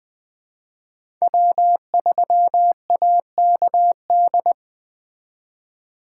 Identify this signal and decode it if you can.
Morse code, heard as a keyed tone, about 20 wpm, 700 Hz. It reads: W3AKD